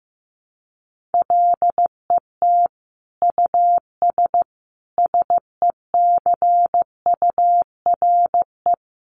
Morse code, heard as a keyed tone, about 15 wpm, 700 Hz. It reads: LET US SECURE
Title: Sentences from Top 1200 Words 15wpm